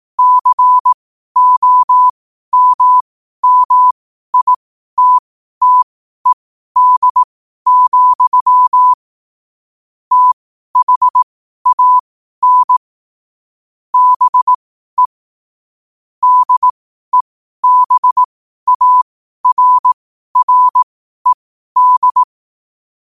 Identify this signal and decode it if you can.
Morse code, heard as a keyed tone, about 18 wpm, spaced as for 12 wpm, 1000 Hz. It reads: COMMITTED, THAN BE DEBARRED
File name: 12wpm